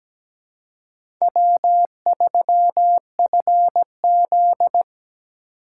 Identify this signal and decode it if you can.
Morse code, heard as a keyed tone, about 17 wpm, 700 Hz. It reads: W3FZ
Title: USA Call Signs - 4 Characters - 8x Character Spacing - 17wpm